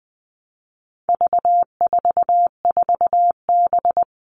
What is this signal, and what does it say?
Morse code, heard as a keyed tone, about 20 wpm, 700 Hz. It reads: V44B
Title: All Cut Numbers - 4-Digit 20wpm